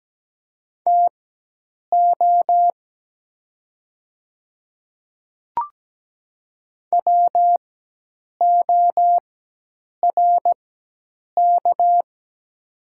Morse code, heard as a keyed tone, about 17 wpm, spaced as for 8 wpm, 700 Hz. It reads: TO WORK
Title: 4x Character Spacing - Top 500 Words - 17wpm